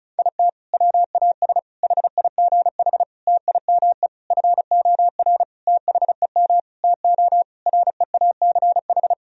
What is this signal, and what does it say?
Morse code, heard as a keyed tone, about 34 wpm, 700 Hz. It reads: IT WAS HIGH TIME FOR THEM TO REACH
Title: Sentences from Top 1000 Words - Review Part 2 - 35wpm